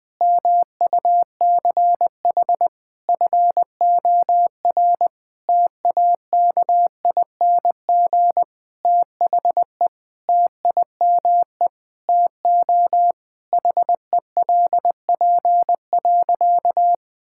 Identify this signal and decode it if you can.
Morse code, heard as a keyed tone, about 19 wpm, 700 Hz. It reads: MUCH FOR TAKING THE TIME TO HELP.
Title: Sentences with 9-Word Phrases - 20wpm